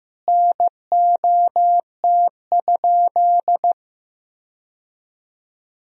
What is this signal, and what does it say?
Morse code, heard as a keyed tone, about 15 wpm, 700 Hz. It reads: NOT?
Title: Sentences from Top 1000 Words - Review Part 3 - 15wpm